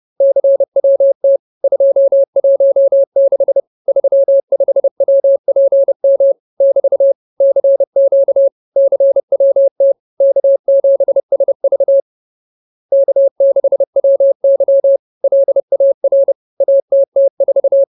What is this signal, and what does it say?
Morse code, heard as a keyed tone, about 30 wpm, 550 Hz. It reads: CWT 216 35WPM <BT> CQ CWT K7SV K6WY LAR ATT4
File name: CWT-216-30